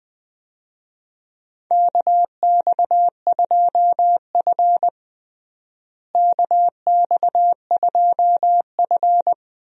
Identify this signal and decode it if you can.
Morse code, heard as a keyed tone, about 20 wpm, 700 Hz. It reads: KX2F KX2F
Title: USA Call Signs - 4 Characters - Repeated 2x - 20wpm